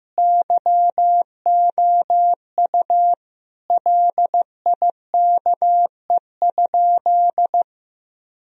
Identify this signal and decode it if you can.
Morse code, heard as a keyed tone, about 15 wpm, 700 Hz. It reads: YOU LIKE?
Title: Sentences from Top 1000 Words - Review Part 3 - 15wpm